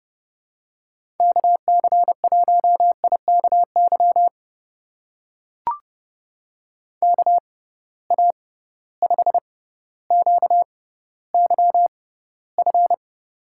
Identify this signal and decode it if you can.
Morse code, heard as a keyed tone, about 30 wpm, spaced as for 15 wpm, 700 Hz. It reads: KC1IKY KA5QYF
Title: USA Call Signs - 6 Characters - 6x Character Spacing - 30wpm